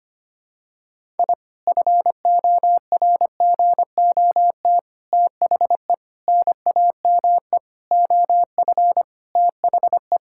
Morse code, heard as a keyed tone, about 25 wpm, 700 Hz. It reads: I FORGOT THE NAME OF THE
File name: practice / 25wpm / Sentences with 4-Word Phrases - 25wpm